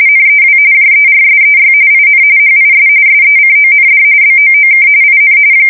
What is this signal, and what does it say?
RTTY, about 45.45 baud. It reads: NEW JERSEY IN HIS SECOND BOOK IN
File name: fsk170 / battleshipnj